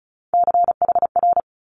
Morse code, heard as a keyed tone, about 36 wpm, 700 Hz. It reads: CHR
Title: Multi-letter Phonemes Quick Fire 35wpm